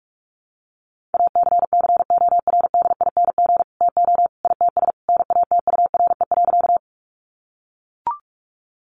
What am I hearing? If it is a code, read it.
Morse code, heard as a keyed tone, about 44 wpm, 700 Hz. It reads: ACCORDING TO ITS NATURE.
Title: Sentences from Top 700 Words 45wpm